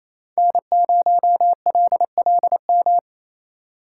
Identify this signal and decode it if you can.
Morse code, heard as a keyed tone, about 35 wpm, 700 Hz. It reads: N0LLM
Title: Speed Racing - USA 1x3 Call Signs 28wpm